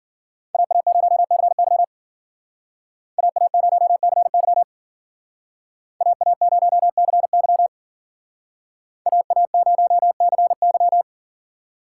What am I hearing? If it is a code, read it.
Morse code, heard as a keyed tone, about 51 wpm, 700 Hz. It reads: AA0CY AA0CY AA0CY AA0CY